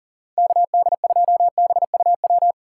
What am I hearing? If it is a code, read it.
Morse code, heard as a keyed tone, about 38 wpm, 700 Hz. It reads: KD2BUW
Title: USA Call Signs - 6 Characters - 4x Character Spacing - 40wpm